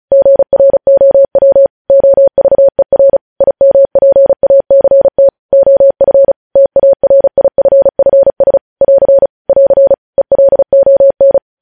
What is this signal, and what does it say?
Morse code, heard as a keyed tone, about 36 wpm, 550 Hz. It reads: GROW OVER IMPACT OF TARIFFS <AR> <AR> ELON
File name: podcast / News headlines-1x-35x35